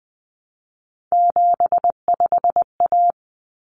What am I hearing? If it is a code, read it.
Morse code, heard as a keyed tone, about 20 wpm, 700 Hz. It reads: 75A
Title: Common Cut Numbers - 3-Digit 20wpm